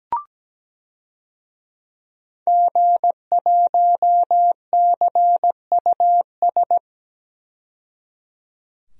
Morse code, heard as a keyed tone, about 17 wpm, 700 Hz. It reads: G1CUS
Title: DXCC-100 Call Signs - 5 Characters - 17wpm